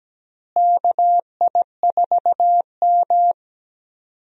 Morse code, heard as a keyed tone, about 17 wpm, 700 Hz. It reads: KI4M
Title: USA Call Signs - 4 Characters - Repeated 2x - 17wpm